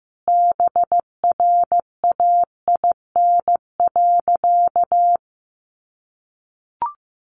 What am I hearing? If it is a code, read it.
Morse code, heard as a keyed tone, about 15 wpm, 700 Hz. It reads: BRAIN.
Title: Sentences from Top 1100 Words 15wpm